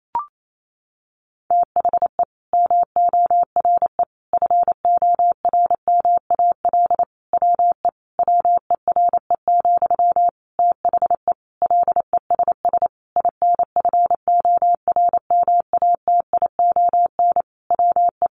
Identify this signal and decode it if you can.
Morse code, heard as a keyed tone, about 28 wpm, 700 Hz. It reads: THE MORE FORMAL WE WERE, THE LESS INFORMATION WE
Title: Sentences from Top 1300 Words 28wpm